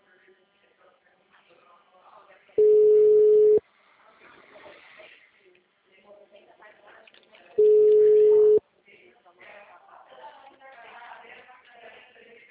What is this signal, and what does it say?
Ringback tone (Europe, 425 Hz), with speech.
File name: out- -2009- - - .6